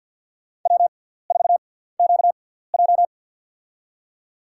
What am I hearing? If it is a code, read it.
Morse code, heard as a keyed tone, about 52 wpm, spaced as for 17 wpm, 700 Hz. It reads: W4QJ
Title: USA Call Signs - 4 Characters - 6x Character Spacing - 50wpm